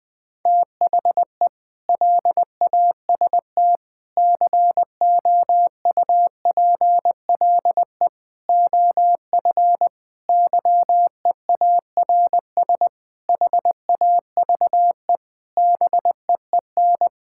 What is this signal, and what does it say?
Morse code, heard as a keyed tone, about 19 wpm, 700 Hz. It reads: THE LAST COUPLE OF YEARS HAVE BEEN
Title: Sentences with 5-Word Phrases - 20wpm